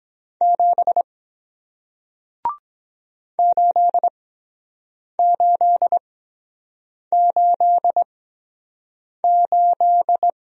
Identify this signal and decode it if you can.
Morse code, heard as a keyed tone, about 23 wpm, spaced as for 7 wpm, 700 Hz. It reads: 7 8888